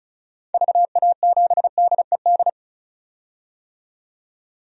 Morse code, heard as a keyed tone, about 36 wpm, 690 Hz. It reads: VA7DED
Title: DXCC-100 Call Signs - 6 Characters - 35wpm